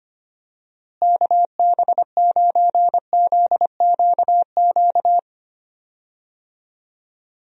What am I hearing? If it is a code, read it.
Morse code, heard as a keyed tone, about 25 wpm, 700 Hz. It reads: KB9ZQQ
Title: USA Call Signs - 6 Characters - Repeated 2x - 25wpm